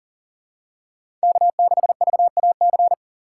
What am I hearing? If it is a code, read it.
Morse code, heard as a keyed tone, about 40 wpm, 700 Hz. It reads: K6VAC
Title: USA Call Signs - 2x Character Spacing - 40wpm